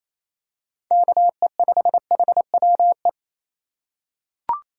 Morse code, heard as a keyed tone, about 28 wpm, 700 Hz. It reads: KE5HWE
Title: USA Call Signs - 6 Characters - 4x Character Spacing - 28wpm